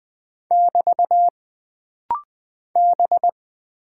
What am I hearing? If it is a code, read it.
Morse code, heard as a keyed tone, about 20 wpm, 700 Hz. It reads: <BT> B
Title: Single Character (Letter:Number:Punctuation:Prosign) - Rapid-Fire - 20wpm